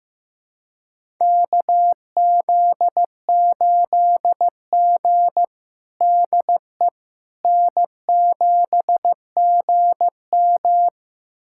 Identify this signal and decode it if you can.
Morse code, heard as a keyed tone, about 15 wpm, 700 Hz. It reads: KZ8G DE N7GM
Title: USA Callsign DE Callsign - 4 Characters - 15wpm